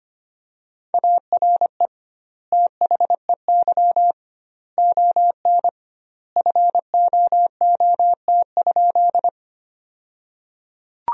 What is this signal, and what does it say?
Morse code, heard as a keyed tone, about 25 wpm, 700 Hz. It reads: ARE THEY ON FOOT?
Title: 2x Word-Spacing - Sentences from Top 500 Words - 25wpm